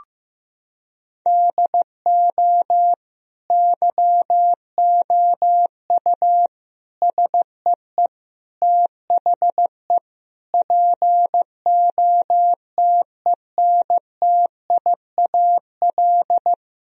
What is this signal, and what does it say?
Morse code, heard as a keyed tone, about 15 wpm, 700 Hz. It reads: DO YOU SEE THE POTENTIAL
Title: Sentences from Top 1000 Words - Review Part 4 - 15wpm